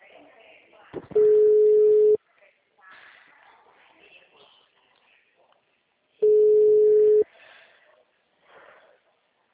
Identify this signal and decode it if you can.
Ringback tone (Europe, 425 Hz), with speech.